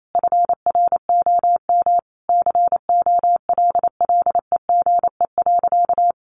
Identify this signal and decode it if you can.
Morse code, heard as a keyed tone, about 28 wpm, 700 Hz. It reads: FROM COLLEGE.
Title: Sentences from Top 1300 Words 28wpm